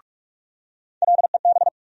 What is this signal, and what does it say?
Morse code, heard as a keyed tone, about 46 wpm, 700 Hz. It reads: LEB
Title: USA Call Sign Postfixes - 3 Letters 45wpm